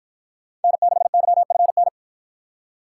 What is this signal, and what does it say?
Morse code, heard as a keyed tone, about 53 wpm, 700 Hz. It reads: N6YFN